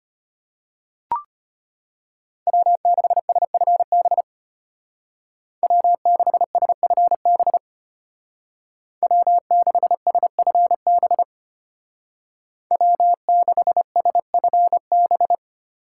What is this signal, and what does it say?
Morse code, heard as a keyed tone, about 32 wpm, 700 Hz. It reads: W6SFB W6SFB W6SFB W6SFB